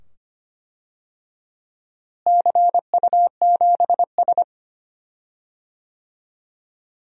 Morse code, heard as a keyed tone, about 25 wpm, 700 Hz. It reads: CU7S
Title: DXCC-100 Call Signs - 4 Characters - Repeated 2x - 25wpm